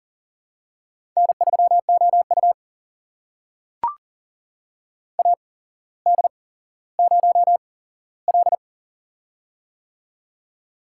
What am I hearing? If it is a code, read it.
Morse code, heard as a keyed tone, about 40 wpm, spaced as for 17 wpm, 700 Hz. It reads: N3OU AD0L
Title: USA Call Signs - 4 Characters - 8x Character Spacing - 40wpm